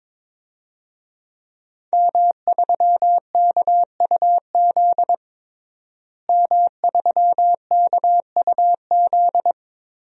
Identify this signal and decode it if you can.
Morse code, heard as a keyed tone, about 22 wpm, 700 Hz. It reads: M3KUZ M3KUZ
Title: DXCC-100 Call Signs - 5 Characters - Repeated 2x - 22wpm